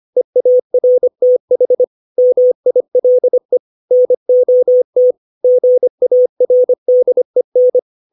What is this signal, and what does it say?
Morse code, heard as a keyed tone, about 25 wpm, 500 Hz. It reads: EARTH MILE NOT GARDEN
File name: 632 words / 632 Words 25WPM